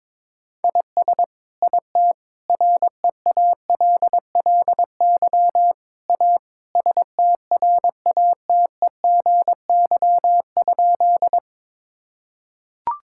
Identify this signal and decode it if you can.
Morse code, heard as a keyed tone, about 22 wpm, 700 Hz. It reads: IS IT REALLY A STRATEGY?
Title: Sentences from Top 1000 Words - Review Part 5 - 22wpm